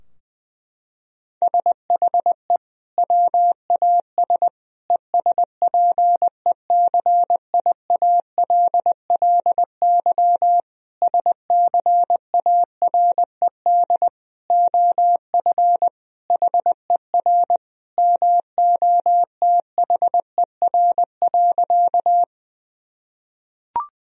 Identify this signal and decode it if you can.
Morse code, heard as a keyed tone, about 20 wpm, 700 Hz. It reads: SHE WAS ESPECIALLY SCARED OF HER MOTHER.
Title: Sentences from Top 1000 Words - Review Part 4 - 20wpm